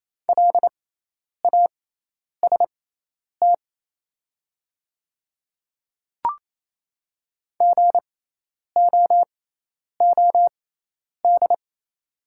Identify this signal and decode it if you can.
Morse code, heard as a keyed tone, about 28 wpm, spaced as for 10 wpm, 700 Hz. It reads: LAST GOOD
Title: 6x Character Spacing - Top 300 Words - 28wpm